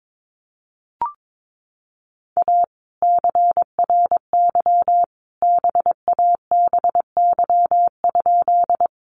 Morse code, heard as a keyed tone, about 22 wpm, 700 Hz. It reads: A CRY BABY?
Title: Sentences from Top 1300 Words 22wpm